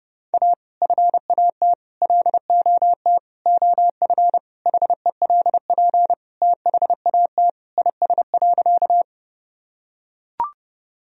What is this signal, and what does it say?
Morse code, heard as a keyed tone, about 30 wpm, 700 Hz. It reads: A FAT LOT OF HELP THAT IS.
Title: Sentences from Top 1000 Words - Review Part 4 - 30wpm